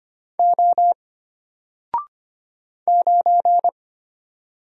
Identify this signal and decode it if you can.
Morse code, heard as a keyed tone, about 25 wpm, 700 Hz. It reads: O 9
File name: Single-Letter-Number-25wpm